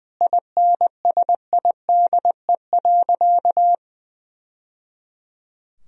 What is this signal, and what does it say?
Morse code, heard as a keyed tone, about 21 wpm, 700 Hz. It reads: INSIDE.
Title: Sentences from Top 1000 Words - Review Part 2 - 20wpm